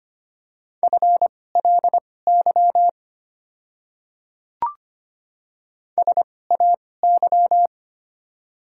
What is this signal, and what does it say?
Morse code, heard as a keyed tone, about 25 wpm, spaced as for 18 wpm, 700 Hz. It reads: FLY SAY
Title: 2x Character Spacing - Top 500 Words - 25wpm